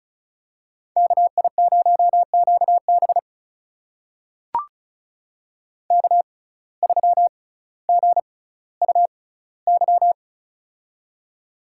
Morse code, heard as a keyed tone, about 36 wpm, spaced as for 18 wpm, 700 Hz. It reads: KI0QB K3GUY
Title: USA Call Signs - 6x Character Spacing - 35wpm